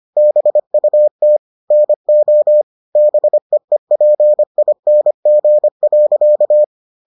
Morse code, heard as a keyed tone, about 25 wpm, 600 Hz. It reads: BUT NO BEEPING.